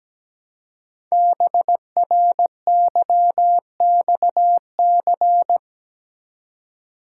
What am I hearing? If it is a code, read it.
Morse code, heard as a keyed tone, about 17 wpm, 700 Hz. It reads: BRYXC
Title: Code Groups - 5 Letters - 17wpm